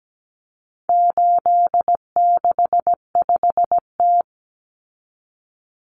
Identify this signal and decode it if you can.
Morse code, heard as a keyed tone, about 17 wpm, 700 Hz. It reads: 865T
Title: Common Cut Numbers - 4-Digit 17wpm